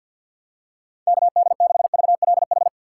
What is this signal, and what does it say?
Morse code, heard as a keyed tone, about 48 wpm, 700 Hz. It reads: KD6VLH
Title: USA Call Signs - 6 Characters - 4x Character Spacing - 50wpm